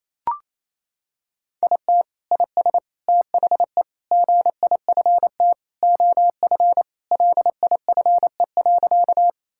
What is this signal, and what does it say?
Morse code, heard as a keyed tone, about 28 wpm, 700 Hz. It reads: IT IS THE GIFT OF LIFE.
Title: Sentences from Top 1000 Words - Review Part 6 - 28wpm